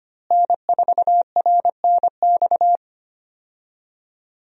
Morse code, heard as a keyed tone, about 25 wpm, 700 Hz. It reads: N4RNX
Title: USA Call Signs - 5 Characters - Repeated 2x - 25wpm